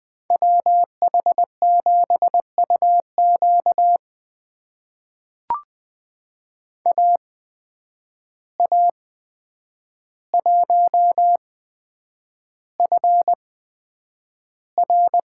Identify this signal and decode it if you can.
Morse code, heard as a keyed tone, about 20 wpm, spaced as for 9 wpm, 700 Hz. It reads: WH7UQ AA1FR
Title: USA Call Signs - 5 Characters - 8x Character Spacing - 20wpm